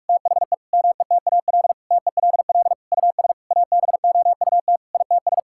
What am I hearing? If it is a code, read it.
Morse code, heard as a keyed tone, about 45 wpm, 700 Hz. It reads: THE METAL TELL US ABOUT ITS
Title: Sentences from Top 1000 Words - Review Part 6 - 45wpm